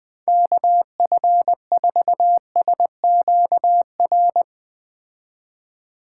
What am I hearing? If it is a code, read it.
Morse code, heard as a keyed tone, about 20 wpm, 700 Hz. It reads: KF4SQR
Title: Speed Racing - USA 2x3 Call Signs 17wpm